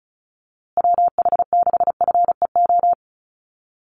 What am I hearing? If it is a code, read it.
Morse code, heard as a keyed tone, about 34 wpm, 700 Hz. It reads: WH6FEO
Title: USA Call Signs Repeated 3x 35wpm